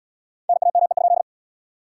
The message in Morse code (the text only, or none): NAME?